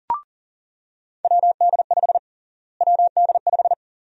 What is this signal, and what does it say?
Morse code, heard as a keyed tone, about 40 wpm, 700 Hz. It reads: WD5 WD5
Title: USA Call Sign Prefixes - Repeated 2x - 40wpm